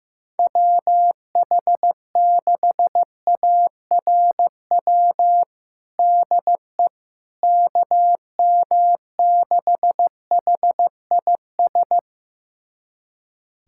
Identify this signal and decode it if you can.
Morse code, heard as a keyed tone, about 15 wpm, 700 Hz. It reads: WH6ARW DE KM6HIS
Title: USA Callsign DE Callsign - 6 Characters - 15wpm